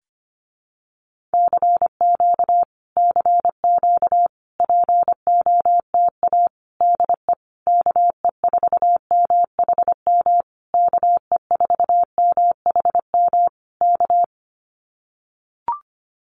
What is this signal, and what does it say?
Morse code, heard as a keyed tone, about 25 wpm, 700 Hz. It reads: CQ CQ POTA DE KE4MHM KE4MHM K
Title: Common QSO Phrases Encore 25wpm